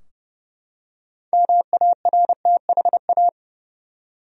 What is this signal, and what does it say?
Morse code, heard as a keyed tone, about 32 wpm, 700 Hz. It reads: MARTHA
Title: USA Common Names - 30wpm